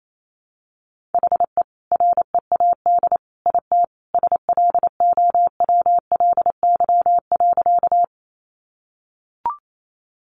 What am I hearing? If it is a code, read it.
Morse code, heard as a keyed tone, about 28 wpm, 700 Hz. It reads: HE READ IT SLOWLY.
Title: Sentences from Top 1100 Words 28wpm